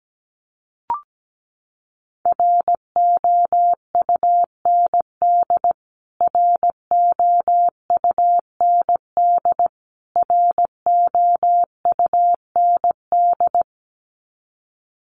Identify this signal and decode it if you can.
Morse code, heard as a keyed tone, about 17 wpm, 700 Hz. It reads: ROUND ROUND ROUND
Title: Top 500 Words Repeated 17wpm